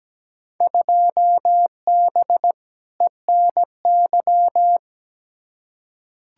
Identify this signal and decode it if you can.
Morse code, heard as a keyed tone, about 17 wpm, 700 Hz. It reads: 2B ENY
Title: Field Day Exchanges Encore - 17wpm